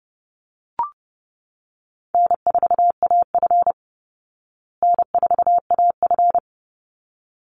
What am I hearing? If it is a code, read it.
Morse code, heard as a keyed tone, about 30 wpm, 700 Hz. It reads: N4AF N4AF
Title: Easy Call Signs Repeated 2x - 30wpm